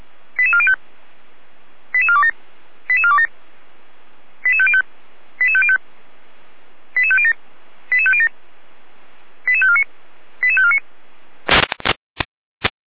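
Five-tone selective call (ZVEI), heard as quick runs of five tones, tones of about 70 ms. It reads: 88405 88427 88427 88505 88505 88507 88507 88549 88549
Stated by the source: Five-tone ZVEI selective call 88447 received by SDR, from the Feuerwehr Orlamünde audio server.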